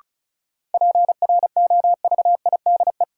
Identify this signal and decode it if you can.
Morse code, heard as a keyed tone, about 36 wpm, 700 Hz. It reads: PROVIDE
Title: Sentences from Top 1000 Words - Review Part 2 - New Word Drill - 35wpm